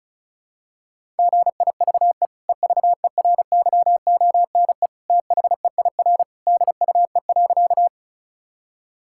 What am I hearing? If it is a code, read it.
Morse code, heard as a keyed tone, about 36 wpm, 700 Hz. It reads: GIVE EVERYONE THEIR DUE.
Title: Sentences from Top 1000 Words - Review Part 6 - 35wpm